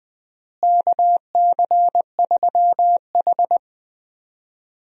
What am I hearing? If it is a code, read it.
Morse code, heard as a keyed tone, about 20 wpm, 700 Hz. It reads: KC3H
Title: USA Call Signs - 4 Characters - Repeated 2x - 20wpm